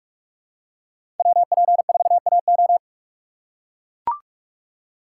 Morse code, heard as a keyed tone, about 44 wpm, 690 Hz. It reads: WP4AO
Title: USA Call Signs - 5 Characters - 6x Character Spacing - 45wpm